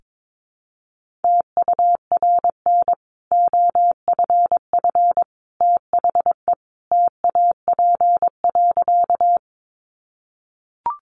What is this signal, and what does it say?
Morse code, heard as a keyed tone, about 22 wpm, 700 Hz. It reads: TURN OFF THE TAP.
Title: Sentences from Top 1400 Words 22wpm